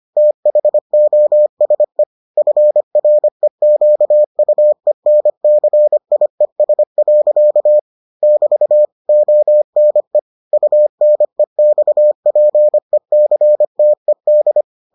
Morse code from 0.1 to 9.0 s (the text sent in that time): THOSE FREQUENCIES. <BT>